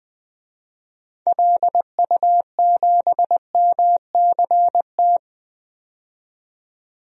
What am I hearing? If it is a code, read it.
Morse code, heard as a keyed tone, about 20 wpm, 700 Hz. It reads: LU7MCT
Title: DXCC-100 Call Signs - 6 Characters - 20wpm